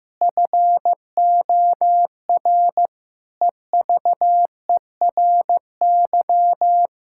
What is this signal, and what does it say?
Morse code, heard as a keyed tone, about 15 wpm, 700 Hz. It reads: FOR EVERY
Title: Sentences from Top 1000 Words - Review Part 5 - 15wpm